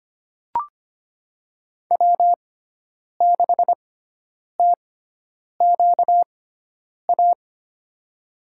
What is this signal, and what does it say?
Morse code, heard as a keyed tone, about 25 wpm, spaced as for 9 wpm, 700 Hz. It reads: W6TQA
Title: USA Call Signs - 5 Characters - 6x Character Spacing - 25wpm